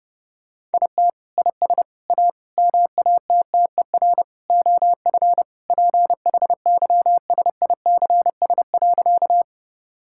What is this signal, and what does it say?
Morse code, heard as a keyed tone, about 30 wpm, 700 Hz. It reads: IT IS A MATTER OF PHYSICS.
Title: Sentences from Top 1000 Words - Review Part 5 - 30wpm